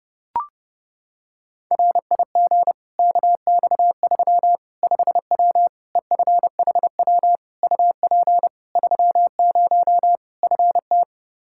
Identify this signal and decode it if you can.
Morse code, heard as a keyed tone, about 30 wpm, 700 Hz. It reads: RIG KX3 5W EFHW UP 30 FT
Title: Practice QSO - Exchange 2 - Info - 30wpm